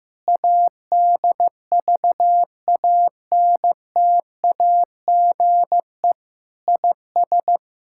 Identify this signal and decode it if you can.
Morse code, heard as a keyed tone, about 15 wpm, 700 Hz. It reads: ADVANTAGE IS
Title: Sentences from Top 1000 Words - Review Part 6 - 15wpm